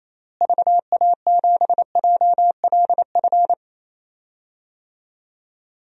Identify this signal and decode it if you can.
Morse code, heard as a keyed tone, about 28 wpm, 700 Hz. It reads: VA7JLF
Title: DXCC-100 Call Signs - 6 Characters - Repeated 2x - 28wpm